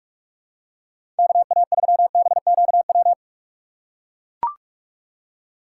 Morse code, heard as a keyed tone, about 43 wpm, 700 Hz. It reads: KA3BQW
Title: USA Call Signs - 6 Characters - 2x Character Spacing - 45wpm